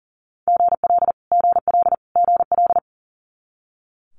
Morse code, heard as a keyed tone, about 40 wpm, 700 Hz. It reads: GL GL GL
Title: Multi-letter Phonemes Repeated 3x 40wpm